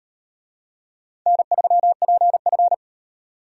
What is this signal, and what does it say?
Morse code, heard as a keyed tone, about 37 wpm, 700 Hz. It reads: N3PF